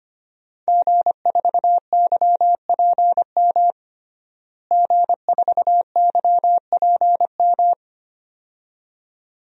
Morse code, heard as a keyed tone, about 25 wpm, 700 Hz. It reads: G4YPM G4YPM
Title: DXCC-100 Call Signs - 5 Characters - Repeated 2x - 25wpm